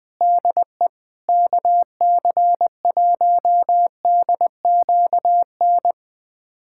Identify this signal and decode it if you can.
Morse code, heard as a keyed tone, about 21 wpm, 700 Hz. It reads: DE KC1DQN
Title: USA Callsign DE Callsign 20wpm